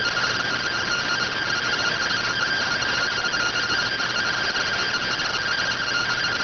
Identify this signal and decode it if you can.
Two-tone teleprinter signal, about 170 Hz shift.